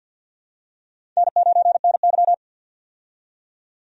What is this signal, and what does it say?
Morse code, heard as a keyed tone, about 53 wpm, 700 Hz. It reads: N9NY
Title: USA Call Signs - 4 Characters - 50wpm